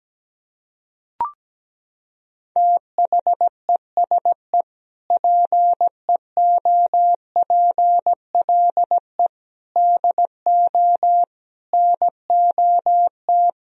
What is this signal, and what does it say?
Morse code, heard as a keyed tone, about 17 wpm, 700 Hz. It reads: THESE PEOPLE DO NOT
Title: Sentences from Top 1000 Words - Review Part 1 - 17wpm